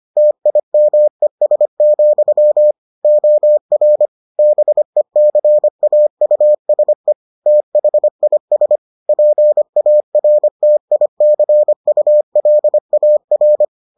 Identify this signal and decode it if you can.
Morse code, heard as a keyed tone, about 25 wpm, 600 Hz. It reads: TIMES, OR BECAUSE THIS PARTICULAR